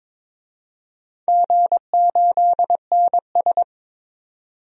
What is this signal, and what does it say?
Morse code, heard as a keyed tone, about 22 wpm, 700 Hz. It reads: G8NS
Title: DXCC-100 Call Signs - 4 Characters - 22wpm